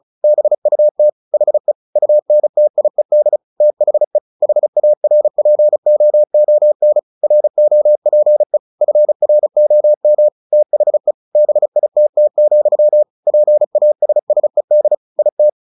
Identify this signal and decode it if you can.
Morse code, heard as a keyed tone, about 34 wpm, 600 Hz. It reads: BUT HE UNTIED THE HARPOON ROPE FROM THE BITT, PASSED IT
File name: the old man and the sea 2 35wpm